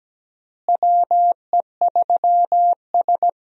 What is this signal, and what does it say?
Morse code, heard as a keyed tone, about 17 wpm, 700 Hz. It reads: WE3S
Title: USA Call Signs - 4 Characters - Repeated 2x - 17wpm